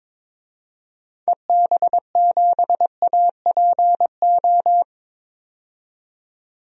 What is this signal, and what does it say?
Morse code, heard as a keyed tone, about 22 wpm, 700 Hz. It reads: EB7APO
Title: DXCC-100 Call Signs - 6 Characters - 22wpm